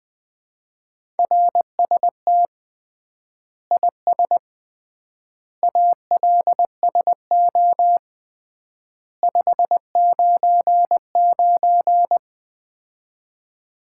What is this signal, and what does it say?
Morse code, heard as a keyed tone, about 21 wpm, 700 Hz. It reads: RST IS ALSO 599
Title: Common QSO Phrases - 3x Word-Spacing - 20wpm